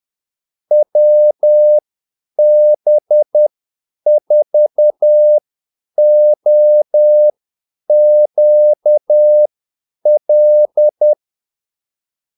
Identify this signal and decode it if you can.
Morse code, heard as a keyed tone, about 10 wpm, spaced as for 8 wpm, 600 Hz. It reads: WB4OQL
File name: Callsigns-10-8wpm